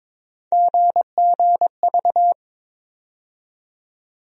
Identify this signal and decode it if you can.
Morse code, heard as a keyed tone, about 22 wpm, 700 Hz. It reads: GGV
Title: USA Call Sign Postfixes - 3 Letters 22wpm